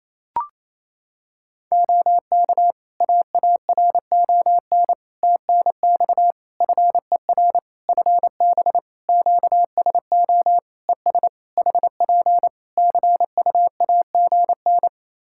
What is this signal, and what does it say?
Morse code, heard as a keyed tone, about 27 wpm, 700 Hz. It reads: OK AARON TNX FER FB QSO ES HP CUAGN
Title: K1USN SST QSO Practice 28wpm